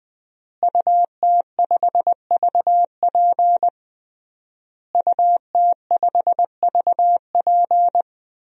Morse code, heard as a keyed tone, about 20 wpm, 700 Hz. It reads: UT5VP UT5VP
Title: DXCC-100 Call Signs - 5 Characters - Repeated 3x - 20wpm